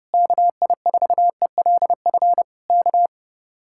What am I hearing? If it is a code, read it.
Morse code, heard as a keyed tone, about 30 wpm, 700 Hz. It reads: KI4ELF K
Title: Speed-Racing Common QSO Phrases Encore - 30wpm